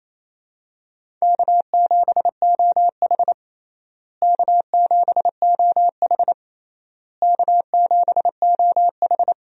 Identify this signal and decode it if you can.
Morse code, heard as a keyed tone, about 28 wpm, 700 Hz. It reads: K7OH K7OH K7OH
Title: USA Call Signs - 4 Characters - Repeated 3x - 28wpm